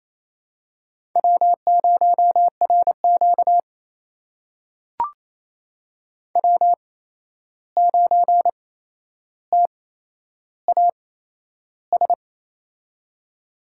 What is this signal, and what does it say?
Morse code, heard as a keyed tone, about 28 wpm, spaced as for 11 wpm, 700 Hz. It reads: W0RQ W9TAS
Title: USA Call Signs - 8x Character Spacing - 28wpm